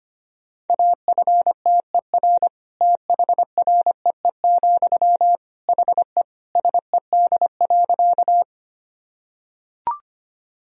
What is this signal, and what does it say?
Morse code, heard as a keyed tone, about 25 wpm, 700 Hz. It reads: AFTER THREE, HE SED.
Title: Sentences from Top 200 Words and Abbreviations 25wpm